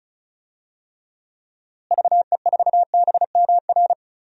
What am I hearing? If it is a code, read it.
Morse code, heard as a keyed tone, about 35 wpm, 700 Hz. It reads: VE4BMR
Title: DXCC-100 Call Signs - 6 Characters - Repeated 3x - 35wpm